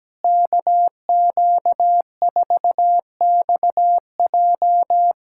Morse code, heard as a keyed tone, about 17 wpm, 700 Hz. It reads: KQ4XJ
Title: USA Callsign DE Callsign - 5 Characters - 17wpm